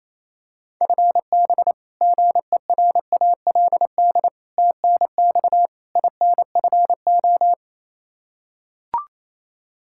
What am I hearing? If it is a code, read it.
Morse code, heard as a keyed tone, about 28 wpm, 700 Hz. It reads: FB GERALD TNX INFO
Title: Practice QSO - Exchange 3 - More Info - 28wpm